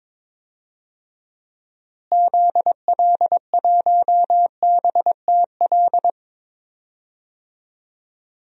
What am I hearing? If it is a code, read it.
Morse code, heard as a keyed tone, about 21 wpm, 700 Hz. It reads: ZL1BTL
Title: DXCC-100 Call Signs - 6 Characters - Repeated 3x - 22wpm